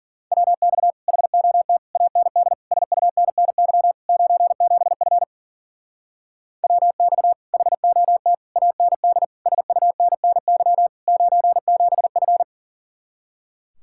Morse code, heard as a keyed tone, about 51 wpm, 700 Hz. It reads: WX HOT AND SUNNY 97F WX HOT AND SUNNY 97F